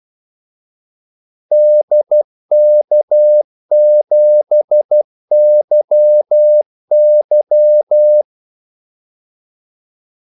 Morse code, heard as a keyed tone, about 12 wpm, 600 Hz. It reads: DK7YY
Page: 2 ekstra ordmellemrum mellem kaldesignalerne